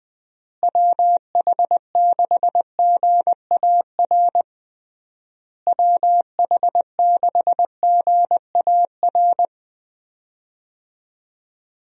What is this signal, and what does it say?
Morse code, heard as a keyed tone, about 20 wpm, 700 Hz. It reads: WH6GAR WH6GAR
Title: USA Call Signs - 6 Characters - Repeated 3x - 20wpm